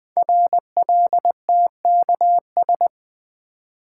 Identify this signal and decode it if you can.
Morse code, heard as a keyed tone, about 20 wpm, 700 Hz. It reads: RLTKS
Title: Code Groups - 5 Letters - 20wpm